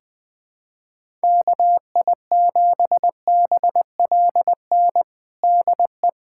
Morse code, heard as a keyed tone, about 21 wpm, 700 Hz. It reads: KI7BLN DE
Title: USA Callsign DE Callsign 20wpm